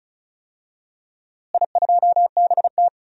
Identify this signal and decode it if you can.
Morse code, heard as a keyed tone, about 36 wpm, 700 Hz. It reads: I2BT
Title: DXCC-100 Call Signs - 4 Characters - Repeated 2x - 35wpm